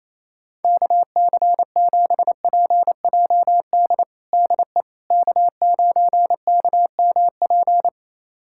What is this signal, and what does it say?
Morse code, heard as a keyed tone, about 28 wpm, 700 Hz. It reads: KC7PJD DE K9KMP
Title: USA Callsign DE Callsign 28wpm